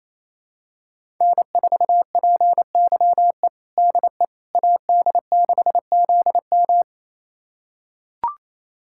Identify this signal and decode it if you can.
Morse code, heard as a keyed tone, about 28 wpm, 700 Hz. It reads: N4PYE DE AD6ZM
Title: USA Callsign DE Callsign - 5 Characters - 28wpm